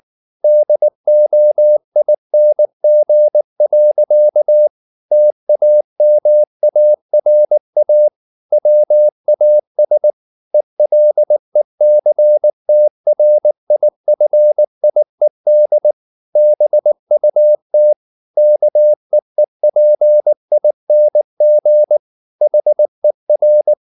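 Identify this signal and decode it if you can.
Morse code, heard as a keyed tone, about 19 wpm, 600 Hz. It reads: DOING. TAMARA WAS ELECTRIFIED BUT KEEPING HER
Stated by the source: Morse code at 19 wpm from CW med Gnister, reading Ken Follett's novel Never in English.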